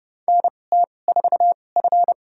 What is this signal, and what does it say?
Morse code, heard as a keyed tone, about 30 wpm, spaced as for 22 wpm, 700 Hz. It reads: NT4F
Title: USA Call Signs - 4 Characters - 2x Character Spacing - 30wpm